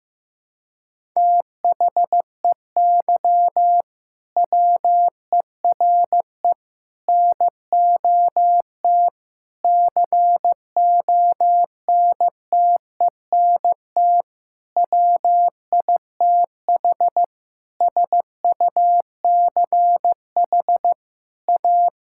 Morse code, heard as a keyed tone, about 16 wpm, 700 Hz. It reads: THEY WERE NOT CONTENT WITH SUCH A
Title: Sentences from Top 1000 Words - Review Part 6 - 15wpm